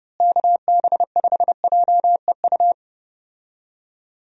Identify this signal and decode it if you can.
Morse code, heard as a keyed tone, about 30 wpm, 700 Hz. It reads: KB5JEU
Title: USA Call Signs - 6 Characters - Repeated 2x - 30wpm